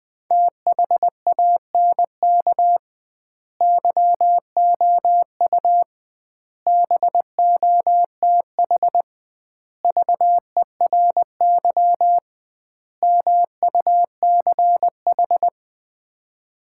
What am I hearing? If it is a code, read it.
Morse code, heard as a keyed tone, about 21 wpm, 700 Hz. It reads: THANK YOU BOTH VERY MUCH
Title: 2x Word-Spacing - Sets of 5 Words - 20wpm